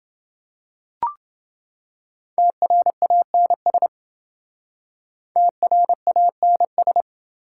Morse code, heard as a keyed tone, about 28 wpm, 700 Hz. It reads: TRANS TRANS